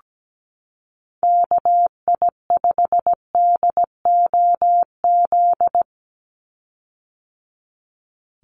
Morse code, heard as a keyed tone, about 17 wpm, 700 Hz. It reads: KI5DOZ
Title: USA Call Signs 17wpm